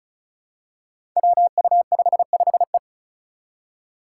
Morse code, heard as a keyed tone, about 36 wpm, 700 Hz. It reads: WU55E
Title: Five Challenging Letter-Number Combinations 35wpm